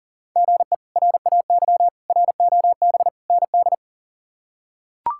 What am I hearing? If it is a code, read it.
Morse code, heard as a keyed tone, about 40 wpm, 700 Hz. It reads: GE RAY ROB ND
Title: K1USN SST QSO Practice - 40wpm